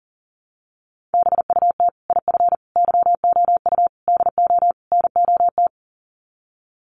Morse code, heard as a keyed tone, about 40 wpm, 700 Hz. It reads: BUT IF YOU DO NOT
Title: Sets of 5 Words Encore 40wpm